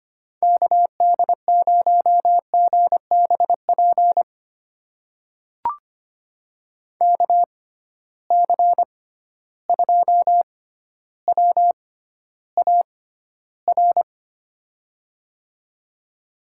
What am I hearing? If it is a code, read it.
Morse code, heard as a keyed tone, about 25 wpm, spaced as for 13 wpm, 700 Hz. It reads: KD0GBP KC2WAR
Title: USA Call Signs - 6 Characters - 6x Character Spacing - 25wpm